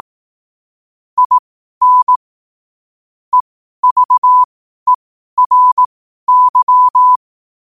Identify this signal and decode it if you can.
Morse code, heard as a keyed tone, about 18 wpm, spaced as for 12 wpm, 1000 Hz. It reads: IN EVERY